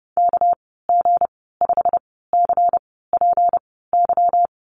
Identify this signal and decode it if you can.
Morse code, heard as a keyed tone, about 30 wpm, spaced as for 17 wpm, 700 Hz. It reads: KG5CPY
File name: Lesson 37g - Letter x - Call Signs 30wpm